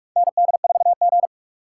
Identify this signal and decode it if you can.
Morse code, heard as a keyed tone, about 46 wpm, 700 Hz. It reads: ND4G
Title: K1USN SST QSO Practice - 45wpm